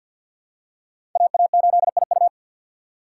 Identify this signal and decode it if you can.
Morse code, heard as a keyed tone, about 53 wpm, 700 Hz. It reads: AA8IU
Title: USA Call Signs - 2x Character Spacing - 50wpm